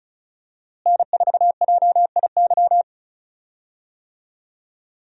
Morse code, heard as a keyed tone, about 34 wpm, 700 Hz. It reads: N4JIY
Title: USA Call Signs - 5 Characters - Repeated 3x - 35wpm